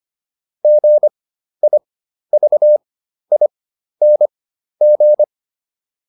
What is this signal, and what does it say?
Morse code, heard as a keyed tone, about 25 wpm, spaced as for 12 wpm, 600 Hz. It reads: GIVING